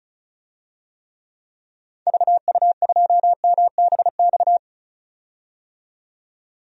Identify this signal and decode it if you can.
Morse code, heard as a keyed tone, about 36 wpm, 700 Hz. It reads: VU2MBX
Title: DXCC-100 Call Signs - 6 Characters - 35wpm